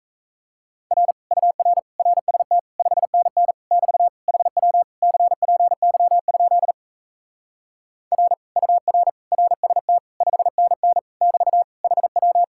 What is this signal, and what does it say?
Morse code, heard as a keyed tone, about 38 wpm, 700 Hz. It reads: R UR RST 5NN <BT> HW CPY? R UR RST 5NN <BT> HW